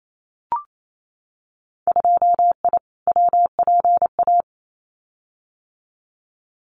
Morse code, heard as a keyed tone, about 28 wpm, 700 Hz. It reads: 2I WPA
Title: Winter Field Day Exchanges 28wpm